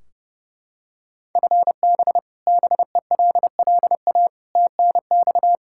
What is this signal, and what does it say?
Morse code, heard as a keyed tone, about 31 wpm, 700 Hz. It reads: FB BELLA TNX
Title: Practice QSO - Exchange 3 - More Info - 30wpm